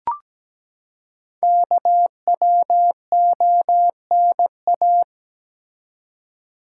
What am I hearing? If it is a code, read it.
Morse code, heard as a keyed tone, about 17 wpm, 700 Hz. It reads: KWONA
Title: Code Groups - 5 Letters - 17wpm